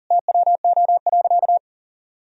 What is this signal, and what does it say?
Morse code, heard as a keyed tone, about 40 wpm, 700 Hz. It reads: TWO.
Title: Sentences from Top 1000 Words - Review Part 1 - 40wpm